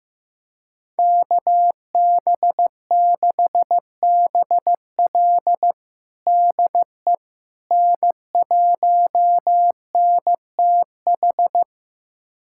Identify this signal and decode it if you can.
Morse code, heard as a keyed tone, about 15 wpm, 700 Hz. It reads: KB6BL DE N1NTH
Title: USA Callsign DE Callsign - 5 Characters - 15wpm